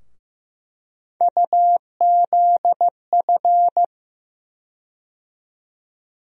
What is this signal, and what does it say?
Morse code, heard as a keyed tone, about 15 wpm, 700 Hz. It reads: UZF
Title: USA Call Sign Postfixes - 3 Letters 15wpm